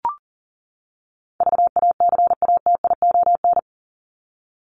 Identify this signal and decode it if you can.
Morse code, heard as a keyed tone, about 40 wpm, 700 Hz. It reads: VACATION
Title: Top 1501-1750 Words 40wpm